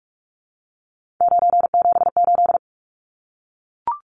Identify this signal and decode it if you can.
Morse code, heard as a keyed tone, about 44 wpm, 700 Hz. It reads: 978
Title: Numbers - 3-Digit 45wpm